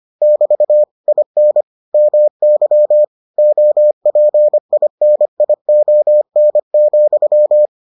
Morse code, heard as a keyed tone, about 25 wpm, 600 Hz. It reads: <BT> IN MY OPINION,